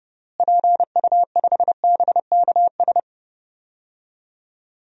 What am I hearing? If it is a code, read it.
Morse code, heard as a keyed tone, about 30 wpm, 700 Hz. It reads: PU5BKS
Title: DXCC-100 Call Signs - 6 Characters - Repeated 3x - 30wpm